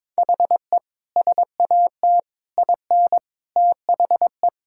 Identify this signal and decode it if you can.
Morse code, heard as a keyed tone, about 22 wpm, 700 Hz. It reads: HE SAT IN THE
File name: practice / 22wpm / Sentences with 3-Word Phrases - 22wpm